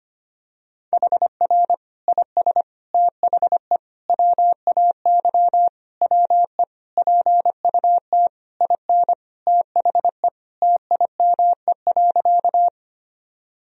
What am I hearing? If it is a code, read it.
Morse code, heard as a keyed tone, about 25 wpm, 700 Hz. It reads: HR IS THE WAY WE PUT IN THE TIME.
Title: Sentences from Top 200 Words and Abbreviations 25wpm